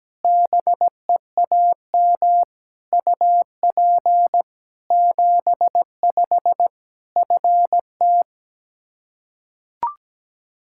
Morse code, heard as a keyed tone, about 17 wpm, 700 Hz. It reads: BEAM UP 75 FT
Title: Practice QSO - Exchange 2 - Info - 17wpm